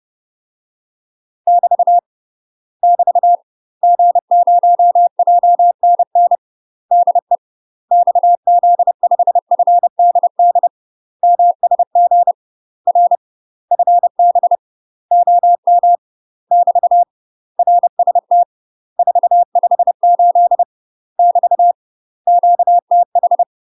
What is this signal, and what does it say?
Morse code, heard as a keyed tone, about 30 wpm, 700 Hz. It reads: <BT> <BT> G0JNN DE XZ5FDD MSG R FB OM <BT> RST 458 <BT> QTH
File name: UK MASTER QSOs TRAINING / 30WPM / UK MASTER QSOs 0050